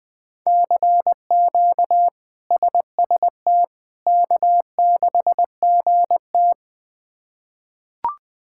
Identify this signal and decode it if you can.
Morse code, heard as a keyed tone, about 20 wpm, 700 Hz. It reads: CQ SST K6GT
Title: K1USN SST QSO Practice - 20wpm